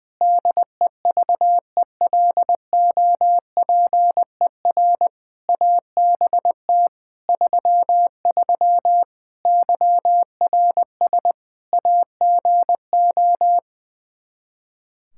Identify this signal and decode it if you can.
Morse code, heard as a keyed tone, about 19 wpm, 700 Hz. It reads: DEVELOPER ABT 33 YRS AGO
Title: K1USN SST QSO Practice 20wpm